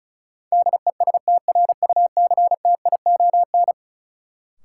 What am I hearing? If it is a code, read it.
Morse code, heard as a keyed tone, about 35 wpm, 700 Hz. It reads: DESTRUCTION
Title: Sets of 5 Words Encore - New Word Drill - 35wpm